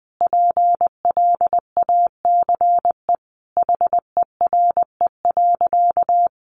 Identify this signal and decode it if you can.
Morse code, heard as a keyed tone, about 20 wpm, 700 Hz. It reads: PLACE HERE.
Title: Sentences from Top 1500 Words 20wpm